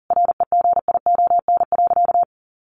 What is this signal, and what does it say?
Morse code, heard as a keyed tone, about 40 wpm, 700 Hz. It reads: REGION.
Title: Sentences from Top 1500 Words 40wpm